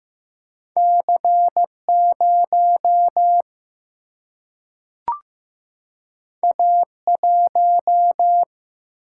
Morse code, heard as a keyed tone, about 15 wpm, 700 Hz. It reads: C 0 A 1
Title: Single-Letter Single-Number 15wpm